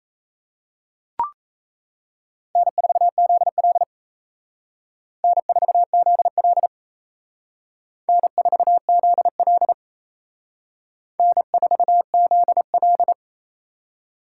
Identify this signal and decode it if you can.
Morse code, heard as a keyed tone, about 35 wpm, 700 Hz. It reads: N4ZL N4ZL N4ZL N4ZL
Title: Speed Racing - USA 1x2 Call Signs 28wpm